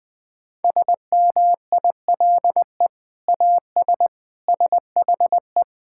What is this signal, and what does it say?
Morse code, heard as a keyed tone, about 20 wpm, 700 Hz. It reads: SMILE AS SHE
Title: Sentences with 2-Word Phrases - Spoken Beginning - 20wpm